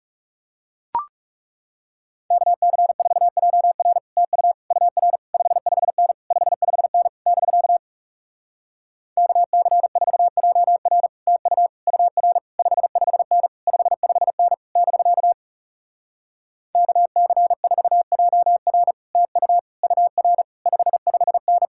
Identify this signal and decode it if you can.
Morse code, heard as a keyed tone, about 40 wpm, 700 Hz. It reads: KC4JR TU UR 55N 55N <BK> KC4JR TU UR 55N 55N <BK> KC4JR TU UR 55N